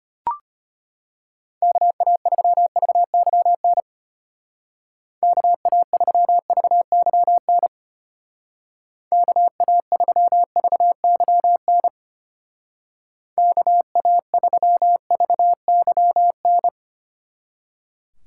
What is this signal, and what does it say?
Morse code, heard as a keyed tone, about 32 wpm, 700 Hz. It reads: KA3VYN KA3VYN KA3VYN KA3VYN